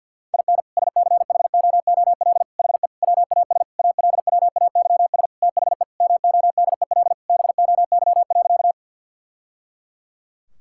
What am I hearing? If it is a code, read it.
Morse code, heard as a keyed tone, about 51 wpm, 700 Hz. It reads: IN SCHOOL HE WAS ALWAYS THE MODEL BOY.
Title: Sentences from Top 1000 Words - Review Part 5 - 50wpm